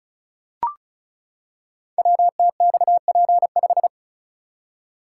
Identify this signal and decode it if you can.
Morse code, heard as a keyed tone, about 36 wpm, 700 Hz. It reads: WTXP5
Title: Code Groups - 5 Letter:Numbers - 35wpm